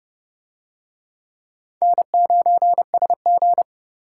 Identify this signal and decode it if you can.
Morse code, heard as a keyed tone, about 28 wpm, 700 Hz. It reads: N9SG